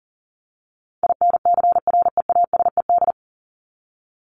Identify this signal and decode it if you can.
Morse code, heard as a keyed tone, about 40 wpm, 700 Hz. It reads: INCREASED
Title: ED Suffix - Top 100 Words with Base Words 40wpm